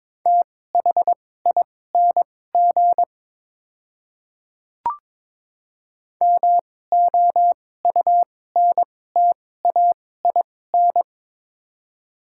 Morse code, heard as a keyed tone, about 22 wpm, spaced as for 16 wpm, 700 Hz. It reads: THING MOUNTAIN
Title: 2x Character Spacing - Top 300 Words - 22wpm